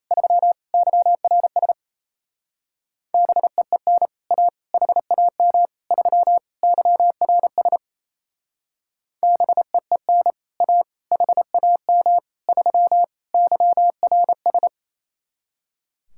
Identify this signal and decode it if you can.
Morse code, heard as a keyed tone, about 33 wpm, 700 Hz. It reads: 3 YRS BEEN A HAM 3 YRS BEEN A HAM 3 YRS
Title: Speed-Racing Common QSO Phrases - 28wpm